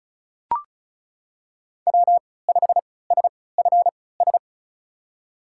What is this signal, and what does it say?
Morse code, heard as a keyed tone, about 36 wpm, spaced as for 20 wpm, 700 Hz. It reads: W5SFS
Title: USA Call Signs - 5 Characters - 3x Character Spacing - 35wpm